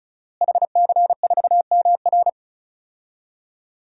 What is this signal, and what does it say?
Morse code, heard as a keyed tone, about 34 wpm, 700 Hz. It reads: HC4MR
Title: DXCC-100 Call Signs - 5 Characters - Repeated 2x - 35wpm